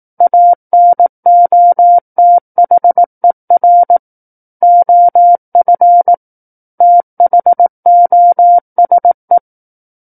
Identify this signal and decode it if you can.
Morse code, heard as a keyed tone, about 18 wpm, 700 Hz. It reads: ANOTHER OF THOSE
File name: 18 WPM